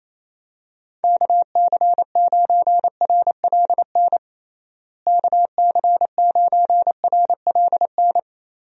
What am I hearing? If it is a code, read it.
Morse code, heard as a keyed tone, about 28 wpm, 700 Hz. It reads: KC9RLN KC9RLN
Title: USA Call Signs - 6 Characters - Repeated 2x - 28wpm